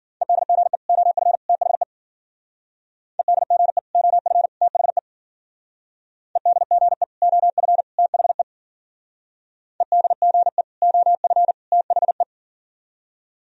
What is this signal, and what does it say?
Morse code, heard as a keyed tone, about 49 wpm, 700 Hz. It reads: EDGE OF THE EDGE OF THE EDGE OF THE EDGE OF THE
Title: Speed-Racing - Sets of 3 Words 40wpm